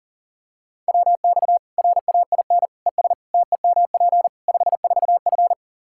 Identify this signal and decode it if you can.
Morse code, heard as a keyed tone, about 37 wpm, 700 Hz. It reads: WX RAIN ES TEMP 54F
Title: Practice QSO - Exchange 2 - Info - 40wpm